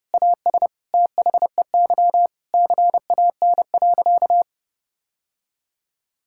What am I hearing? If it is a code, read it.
Morse code, heard as a keyed tone, about 32 wpm, 700 Hz. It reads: AS THEY CAN.
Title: Sentences from Top 1000 Words - Review Part 5 - 30wpm